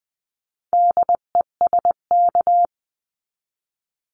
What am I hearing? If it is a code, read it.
Morse code, heard as a keyed tone, about 20 wpm, 700 Hz. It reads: DESK
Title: 259 Lesson 42d - Words 18-20wpm